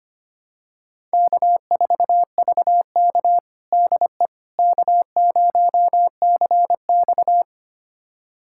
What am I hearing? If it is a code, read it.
Morse code, heard as a keyed tone, about 25 wpm, 700 Hz. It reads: K4VK DE K0CX
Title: USA Callsign DE Callsign - 4 Characters - 25wpm